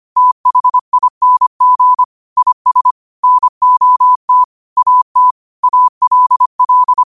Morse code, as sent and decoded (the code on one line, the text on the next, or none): - .... .. -. --. / .. ... / -. --- - / .- - / .- .-.. .-..
THING IS NOT AT ALL